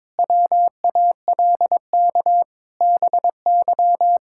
Morse code, heard as a keyed tone, about 22 wpm, 700 Hz. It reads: WALK BY
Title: Sentences from Top 1600 Words 22wpm